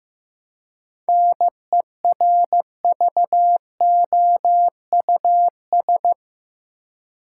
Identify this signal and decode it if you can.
Morse code, heard as a keyed tone, about 15 wpm, 700 Hz. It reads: NERVOUS
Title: Seven Letter Words - 15wpm